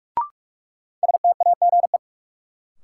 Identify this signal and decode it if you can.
Morse code, heard as a keyed tone, about 46 wpm, 700 Hz. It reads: STAGE
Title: Sentences from Top 1200 Words - New Word Drill Alt - 45wpm